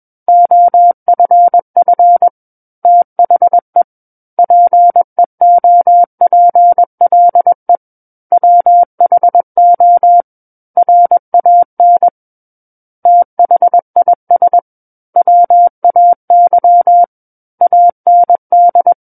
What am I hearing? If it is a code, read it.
Morse code, heard as a keyed tone, about 21 wpm, 700 Hz. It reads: OFF THE PEOPLE WHO RAN THIS WAY AND
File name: Book One / 12-Chapter 12 - 21 WPM